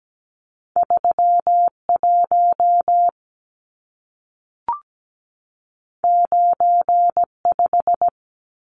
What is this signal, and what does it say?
Morse code, heard as a keyed tone, about 17 wpm, 700 Hz. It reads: 31 95
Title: Numbers - 2-Digit 17wpm